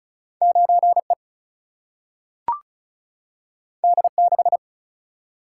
Morse code, heard as a keyed tone, about 36 wpm, 700 Hz. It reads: 9E D6
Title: Single-Letter Single-Number 35wpm